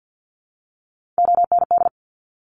Morse code, heard as a keyed tone, about 50 wpm, 700 Hz. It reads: OND
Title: Top Three Letter Combinations in Words 50wpm